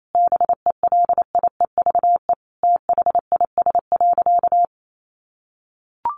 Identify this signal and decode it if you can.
Morse code, heard as a keyed tone, about 28 wpm, 700 Hz. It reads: BELIEVE THIS.
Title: Sentences from Top 1300 Words 28wpm